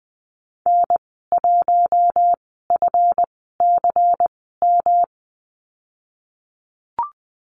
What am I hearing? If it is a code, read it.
Morse code, heard as a keyed tone, about 20 wpm, spaced as for 15 wpm, 700 Hz. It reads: N1FCM
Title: 173 Lesson 28g - Number 9 - Call Signs 20wpm